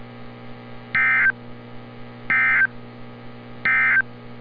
Emergency Alert System (SAME) tones, heard as harsh buzzing bursts. It is heard from 0.8 to 4.1 s.
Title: NOAA Weather Radio - Altoona, PA